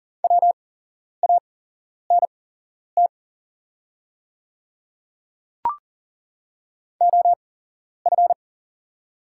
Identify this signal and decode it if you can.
Morse code, heard as a keyed tone, about 41 wpm, spaced as for 11 wpm, 700 Hz. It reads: WANT OF
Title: 8x Character Spacing - Top 500 Words - 40wpm